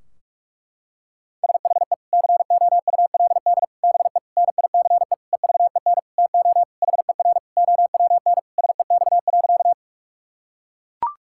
Morse code, heard as a keyed tone, about 41 wpm, 700 Hz. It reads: SHE COULD BE NICE EVEN TO HER OWN SEX.
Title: Sentences from Top 1600 Words 45wpm